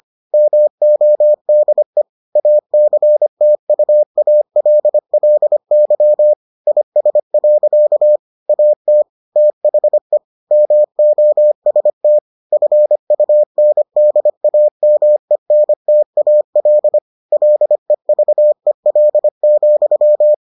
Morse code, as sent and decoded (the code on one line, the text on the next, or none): -- --- -.. / .- -.-. - ..- .- .-.. .-.. -.-- / .. ... .-.-.- / .- - / - .... / -- --- ... - / ..-. ..- -. -.. .- -- . -. - .- .-.. / .-.. ...- . .-.. --..--
MODE ACTUALLY IS. AT THE MOST FUNDAMENTAL LEVEL,